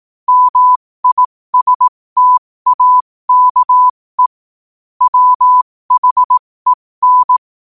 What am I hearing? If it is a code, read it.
Morse code, heard as a keyed tone, about 18 wpm, spaced as for 15 wpm, 1000 Hz. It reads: MISTAKE WHEN
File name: 2024-11-18-15wpm